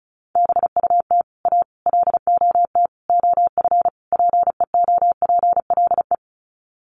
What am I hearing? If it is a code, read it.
Morse code, heard as a keyed tone, about 34 wpm, 700 Hz. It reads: BUT A LOT OF PEOPLE
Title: Sets of 5 Words Encore2 35wpm